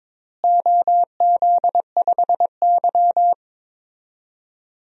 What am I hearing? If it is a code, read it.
Morse code, heard as a keyed tone, about 22 wpm, 700 Hz. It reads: OZ5Y
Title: DXCC-100 Call Signs - 4 Characters - Repeated 2x - 22wpm